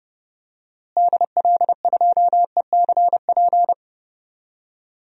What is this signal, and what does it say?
Morse code, heard as a keyed tone, about 31 wpm, 700 Hz. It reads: DL2ECP
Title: DXCC-100 Call Signs - 6 Characters - Repeated 3x - 30wpm